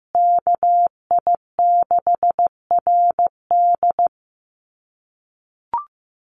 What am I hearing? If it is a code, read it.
Morse code, heard as a keyed tone, about 15 wpm, 700 Hz. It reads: KI6RD
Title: USA Call Signs Repeated 2x 15wpm